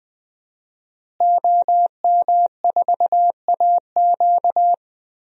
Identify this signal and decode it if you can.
Morse code, heard as a keyed tone, about 20 wpm, 700 Hz. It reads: OM4AQ
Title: DXCC-100 Call Signs - 5 Characters - Repeated 3x - 20wpm